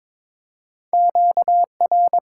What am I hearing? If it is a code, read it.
Morse code, heard as a keyed tone, about 22 wpm, 700 Hz. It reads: QR
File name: Top 90 QSO Elements - 8x Character Spacing - 22wpm